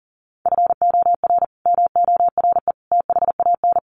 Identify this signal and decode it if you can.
Morse code, heard as a keyed tone, about 40 wpm, 700 Hz. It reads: FOR MORE THAN
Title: Sets of 3 Words 40wpm